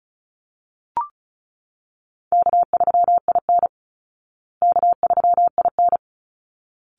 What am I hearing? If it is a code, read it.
Morse code, heard as a keyed tone, about 34 wpm, 700 Hz. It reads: K3IN K3IN
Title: Easy Call Signs Repeated 2x - 35wpm